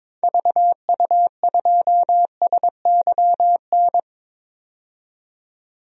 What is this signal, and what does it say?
Morse code, heard as a keyed tone, about 22 wpm, 700 Hz. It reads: VU2SYN
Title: DXCC-100 Call Signs - 6 Characters - 22wpm